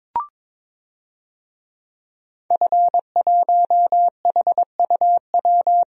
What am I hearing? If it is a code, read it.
Morse code, heard as a keyed tone, about 22 wpm, 700 Hz. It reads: F1HUW
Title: DXCC-100 Call Signs - 5 Characters - Repeated 3x - 22wpm